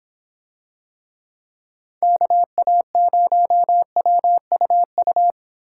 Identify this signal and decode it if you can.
Morse code, heard as a keyed tone, about 25 wpm, 700 Hz. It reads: KA0WUU